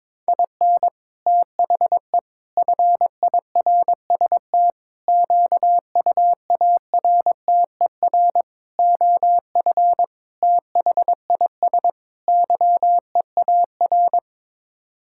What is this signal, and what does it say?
Morse code, heard as a keyed tone, about 22 wpm, 700 Hz. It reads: IN THE FIRST QUARTER OF THIS YEAR
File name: practice / 22wpm / Sets of 7 Words - 22wpm